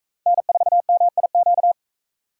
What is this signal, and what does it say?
Morse code, heard as a keyed tone, about 41 wpm, 700 Hz. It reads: N4MIQ